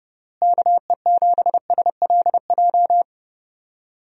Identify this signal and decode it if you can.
Morse code, heard as a keyed tone, about 32 wpm, 700 Hz. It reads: KE7SLJ
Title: USA Call Signs - 6 Characters - Repeated 3x - 30wpm